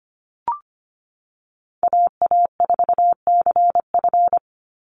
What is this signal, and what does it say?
Morse code, heard as a keyed tone, about 25 wpm, 700 Hz. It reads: AA4CF
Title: Call Signs Repeated 2x - 25wpm